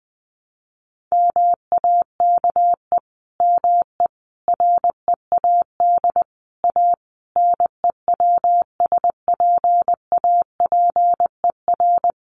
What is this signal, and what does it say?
Morse code, heard as a keyed tone, about 20 wpm, 700 Hz. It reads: MAKE ME READ A NEWSPAPER
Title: Sentences from Top 1100 Words 20wpm